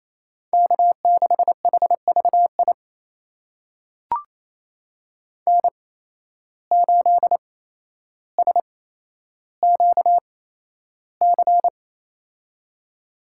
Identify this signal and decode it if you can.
Morse code, heard as a keyed tone, about 28 wpm, spaced as for 12 wpm, 700 Hz. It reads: K6HVI N8SQC
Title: USA Call Signs - 5 Characters - 8x Character Spacing - 28wpm